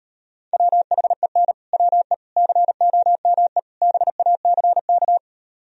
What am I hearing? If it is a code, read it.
Morse code, heard as a keyed tone, about 38 wpm, 700 Hz. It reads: WHEN WE COME BACK